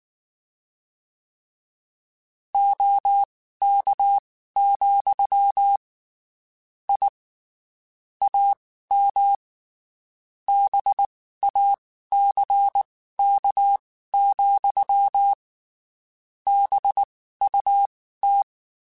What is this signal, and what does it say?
Morse code, heard as a keyed tone, about 19 wpm, spaced as for 14 wpm, 800 Hz. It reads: OK, I AM BACK, BUT
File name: (spaced) / QSOs Spaced-mp3s / kc7haa @18 sp